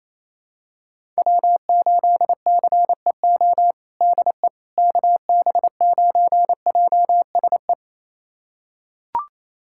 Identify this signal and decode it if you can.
Morse code, heard as a keyed tone, about 27 wpm, 700 Hz. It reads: W8CEO DE KB9JSE
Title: USA Callsign DE Callsign 28wpm